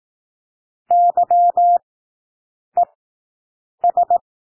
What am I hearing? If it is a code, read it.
Morse code, heard as a keyed tone, about 18 wpm, spaced as for 7 wpm, 700 Hz. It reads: YES
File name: SOUND WORDS 18 wpm